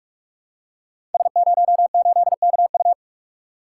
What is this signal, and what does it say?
Morse code, heard as a keyed tone, about 42 wpm, 700 Hz. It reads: S08KU
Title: Code Groups - 5 Letter:Numbers - 45wpm